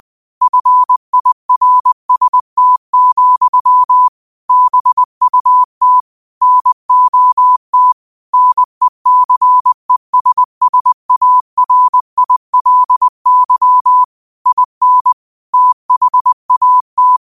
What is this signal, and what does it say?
Morse code, heard as a keyed tone, about 20 wpm, 1000 Hz. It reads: FIRST, BUT NOT NECESSARILY IN THAT